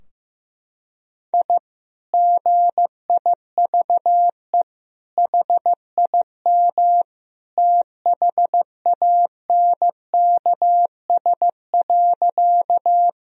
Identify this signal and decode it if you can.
Morse code, heard as a keyed tone, about 15 wpm, 700 Hz. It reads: I GIVE HIM THANKS.
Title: Sentences from Top 1000 Words - Review Part 3 - 15wpm